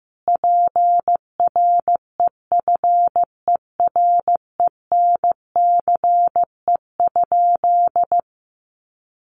Morse code, heard as a keyed tone, about 15 wpm, 700 Hz. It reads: PREFERENCE?
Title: Sentences from Top 1500 Words 15wpm